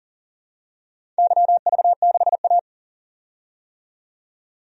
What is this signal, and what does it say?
Morse code, heard as a keyed tone, about 38 wpm, 700 Hz. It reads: YV6A
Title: DXCC-100 Call Signs - 4 Characters - 40wpm